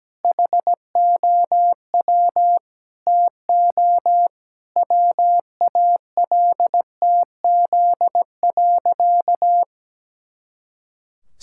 Morse code, heard as a keyed tone, about 17 wpm, 700 Hz. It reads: HOW TO WALTZ.
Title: Pangram Challenge v2 - 17wpm